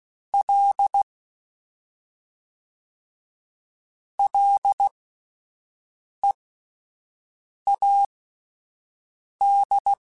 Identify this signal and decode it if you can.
Morse code, heard as a keyed tone, about 16 wpm, spaced as for 5 wpm, 800 Hz. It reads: L LEAD
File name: Ls / Single L m3s / Single L Prefixe 3x2 Twice @5